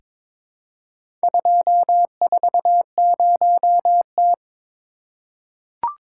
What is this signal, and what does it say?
Morse code, heard as a keyed tone, about 21 wpm, 700 Hz. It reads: 240T
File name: practice / 22wpm / Four Letter-Number Combinations - 22wpm